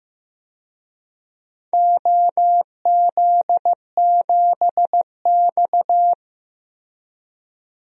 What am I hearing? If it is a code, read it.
Morse code, heard as a keyed tone, about 15 wpm, 700 Hz. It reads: OZ7X
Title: DXCC-100 Call Signs - 4 Characters - 15wpm